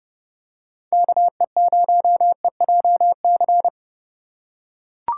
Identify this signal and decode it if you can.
Morse code, heard as a keyed tone, about 30 wpm, 700 Hz. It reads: KE0EJC